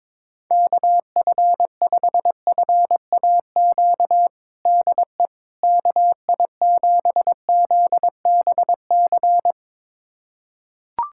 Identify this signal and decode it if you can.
Morse code, heard as a keyed tone, about 22 wpm, 700 Hz. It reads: KF5FAQ DE KI7ZBC
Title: USA Callsign DE Callsign 22wpm